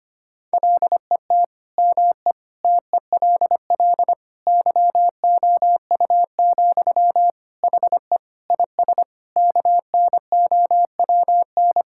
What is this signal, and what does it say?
Morse code, heard as a keyed tone, about 25 wpm, 700 Hz. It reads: LET ME TELL YOU, HE IS KNOWN
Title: Sentences from Top 1600 Words 25wpm